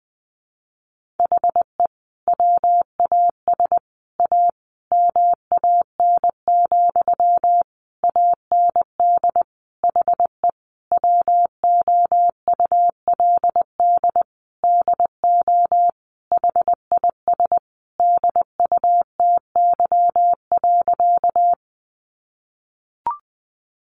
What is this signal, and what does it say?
Morse code, heard as a keyed tone, about 20 wpm, 700 Hz. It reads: HE WAS A MAN, AND HE WOULD DO HIS DUTY.
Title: Sentences from Top 1500 Words 20wpm